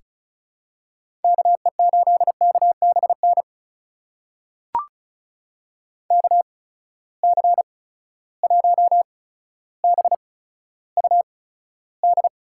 Morse code, heard as a keyed tone, about 36 wpm, spaced as for 15 wpm, 700 Hz. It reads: KE8KBN KC1BUD
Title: USA Call Signs - 6 Characters - 8x Character Spacing - 35wpm